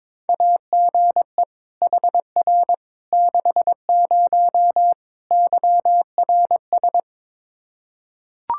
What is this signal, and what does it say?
Morse code, heard as a keyed tone, about 22 wpm, 700 Hz. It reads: AGE HR 60 YRS
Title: Practice QSO - Exchange 3 - More Info - 22wpm